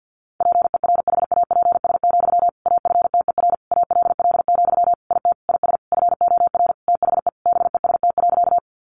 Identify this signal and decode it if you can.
Morse code, heard as a keyed tone, about 50 wpm, 700 Hz. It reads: PERHAPS, AFTER ALL, IT IS FOR THE BEST.
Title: Sentences from Top 500 Words 50wpm